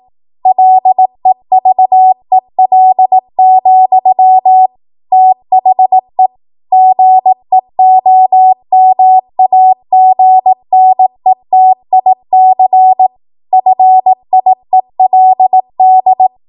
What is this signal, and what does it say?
Morse code, heard as a keyed tone, about 18 wpm, 750 Hz. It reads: LEVEL, THE GEOMAGNETIC FIELD
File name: Morse / 18WPM